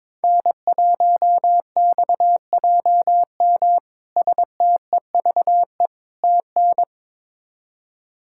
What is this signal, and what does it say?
Morse code, heard as a keyed tone, about 22 wpm, 700 Hz. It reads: N1XJM STEVE TN
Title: K1USN SST QSO Practice - 22wpm